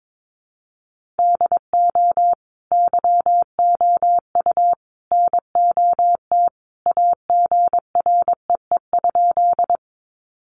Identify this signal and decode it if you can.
Morse code, heard as a keyed tone, about 22 wpm, 700 Hz. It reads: DO YOU NOT AGREE?
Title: Sentences from Top 1100 Words 22wpm